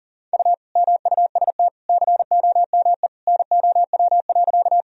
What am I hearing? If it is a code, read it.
Morse code, heard as a keyed tone, about 37 wpm, 700 Hz. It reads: U MUST COME NOW.
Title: Sentences from Top 200 Words and Abbreviations 40wpm